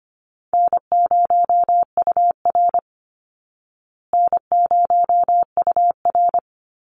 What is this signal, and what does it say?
Morse code, heard as a keyed tone, about 25 wpm, 700 Hz. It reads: N0UR N0UR
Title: Call Signs Repeated 3x - 25wpm